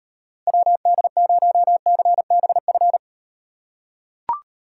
Morse code, heard as a keyed tone, about 38 wpm, 700 Hz. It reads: WD0CBF